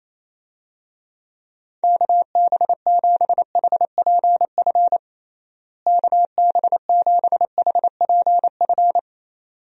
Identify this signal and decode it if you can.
Morse code, heard as a keyed tone, about 28 wpm, 700 Hz. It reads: KB7HPF KB7HPF
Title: USA Call Signs - 6 Characters - Repeated 2x - 28wpm